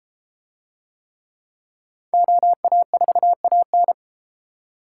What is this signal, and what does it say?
Morse code, heard as a keyed tone, about 31 wpm, 700 Hz. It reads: OA4AN